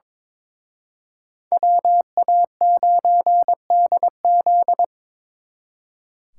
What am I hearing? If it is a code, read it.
Morse code, heard as a keyed tone, about 26 wpm, 700 Hz. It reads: WA9DZ
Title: Speed Racing - USA 2x2 Call Signs 22wpm